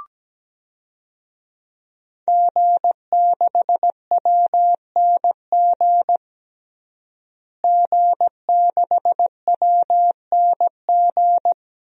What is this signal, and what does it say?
Morse code, heard as a keyed tone, about 17 wpm, 700 Hz. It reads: G6WNG G6WNG
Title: DXCC-100 Call Signs - 5 Characters - Repeated 3x - 17wpm